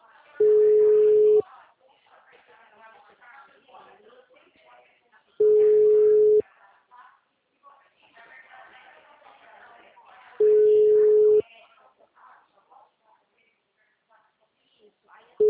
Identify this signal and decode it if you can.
Ringback tone (Europe, 425 Hz), with speech.